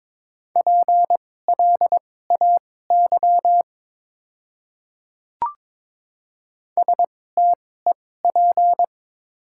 Morse code, heard as a keyed tone, about 21 wpm, spaced as for 16 wpm, 700 Hz. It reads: PLAY STEP
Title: 2x Character Spacing - Top 500 Words - 22wpm